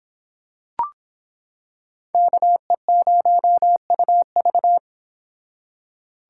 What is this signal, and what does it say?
Morse code, heard as a keyed tone, about 26 wpm, 700 Hz. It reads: KE0UV